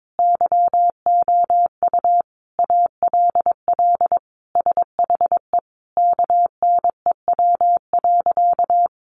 Morse code, heard as a keyed tone, about 22 wpm, 700 Hz. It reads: YOU ALL SHE KNEW.
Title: Sentences from Top 1300 Words 22wpm